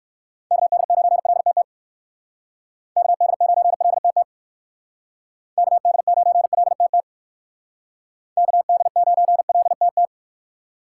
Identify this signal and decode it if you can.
Morse code, heard as a keyed tone, about 55 wpm, 700 Hz. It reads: KD9LTT KD9LTT KD9LTT KD9LTT
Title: Speed Racing - USA 2x3 Call Signs 45wpm